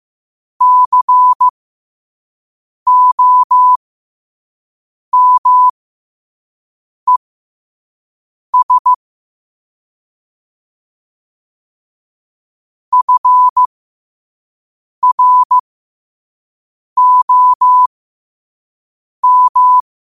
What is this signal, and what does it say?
Morse code, heard as a keyed tone, about 15 wpm, spaced as for 5 wpm, 1000 Hz. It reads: COMES FROM
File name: qotdfiles / 5wpm